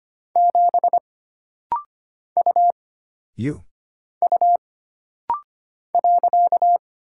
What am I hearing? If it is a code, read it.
Morse code, heard as a keyed tone, about 25 wpm, 700 Hz. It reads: 7 U U .
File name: Single Character (Letter:Number:Punctuation:Prosign) - Rapid-Fire - 25wpm